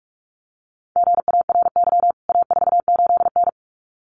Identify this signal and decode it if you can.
Morse code, heard as a keyed tone, about 44 wpm, 700 Hz. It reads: GARY A48N
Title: CWT Exchange 45wpm